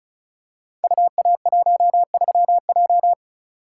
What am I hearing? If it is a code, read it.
Morse code, heard as a keyed tone, about 36 wpm, 700 Hz. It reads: UA13J
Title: Code Groups - 5 Letter:Numbers - 35wpm